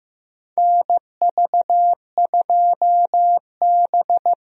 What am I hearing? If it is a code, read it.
Morse code, heard as a keyed tone, about 15 wpm, 700 Hz. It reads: NV2B
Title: USA Call Signs - 4 Characters - Repeated 2x - 15wpm